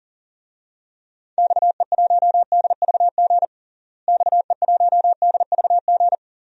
Morse code, heard as a keyed tone, about 40 wpm, 700 Hz. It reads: XE1DVG XE1DVG
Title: DXCC-100 Call Signs - 6 Characters - Repeated 3x - 40wpm